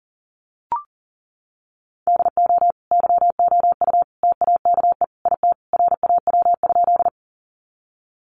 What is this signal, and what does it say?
Morse code, heard as a keyed tone, about 40 wpm, 700 Hz. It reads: DO YOU TAKE IT RAW?
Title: Sentences from Top 1500 Words 40wpm